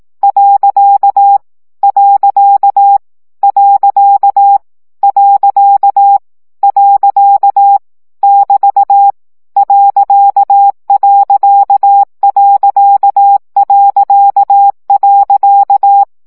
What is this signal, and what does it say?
Morse code, heard as a keyed tone, about 18 wpm, 800 Hz. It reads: . . . <BT> .....
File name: CW gyakorlas pont 15 baud